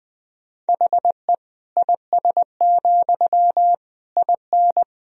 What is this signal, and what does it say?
Morse code, heard as a keyed tone, about 20 wpm, 700 Hz. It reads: HE IS, IN
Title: Sentences from Top 1000 Words - Review Part 3 - 20wpm